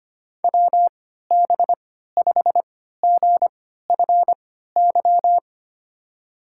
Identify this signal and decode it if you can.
Morse code, heard as a keyed tone, about 25 wpm, spaced as for 14 wpm, 700 Hz. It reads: WB5GFY
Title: USA Call Signs - 3x Character Spacing - 25wpm